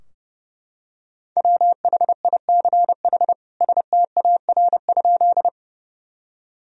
Morse code, heard as a keyed tone, about 30 wpm, 700 Hz. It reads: WHICH STAR?
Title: Sentences from Top 1000 Words - Review Part 2 - 30wpm